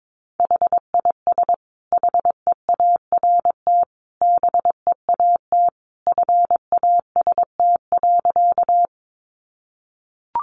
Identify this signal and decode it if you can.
Morse code, heard as a keyed tone, about 22 wpm, 700 Hz. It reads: HIS HEART BEAT FAST.
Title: Sentences from Top 1300 Words 22wpm